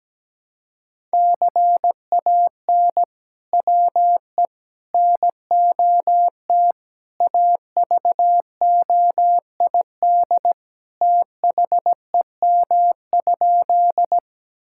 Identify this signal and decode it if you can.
Morse code, heard as a keyed tone, about 17 wpm, 700 Hz. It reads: CAN WE NOT AVOID THEM?
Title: Sentences from Top 1000 Words - Review Part 6 - 17wpm